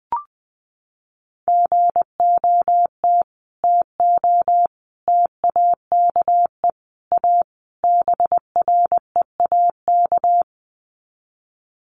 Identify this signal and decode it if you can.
Morse code, heard as a keyed tone, about 20 wpm, 700 Hz. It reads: GOT TO TAKE A BREAK
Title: Sets of 5 Words Encore 20wpm